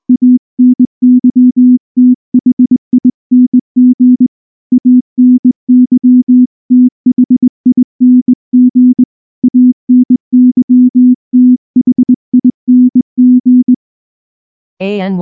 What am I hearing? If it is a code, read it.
Morse code, heard as a keyed tone, about 20 wpm, 250 Hz. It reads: ANYTHING ANYTHING ANYTHING
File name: morse / morse-training-20WPM-7